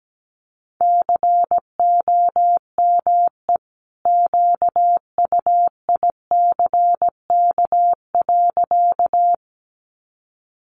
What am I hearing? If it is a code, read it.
Morse code, heard as a keyed tone, about 17 wpm, 700 Hz. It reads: COME QUICK.
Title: Sentences from Top 500 Words 17wpm